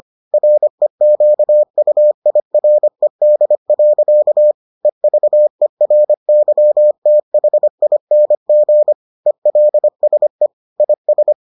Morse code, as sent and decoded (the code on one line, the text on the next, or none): .-. --.- ..- .. .-. -.. .-.-.- / . ...- . .-. -.-- - .... .. -. --. / . .-.. ... / .. ...
REQUIRED. EVERYTHING ELSE IS